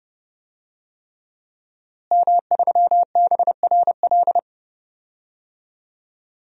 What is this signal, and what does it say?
Morse code, heard as a keyed tone, about 30 wpm, 700 Hz. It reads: M3BRL
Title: DXCC-100 Call Signs - 5 Characters - Repeated 3x - 30wpm